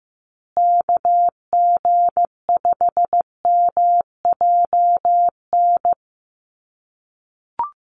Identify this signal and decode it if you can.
Morse code, heard as a keyed tone, about 15 wpm, 700 Hz. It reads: KG5MJN
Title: USA Call Signs Repeated 2x 15wpm